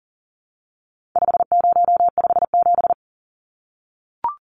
Morse code, heard as a keyed tone, about 40 wpm, 700 Hz. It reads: 5057
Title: Numbers - 4-Digit 40wpm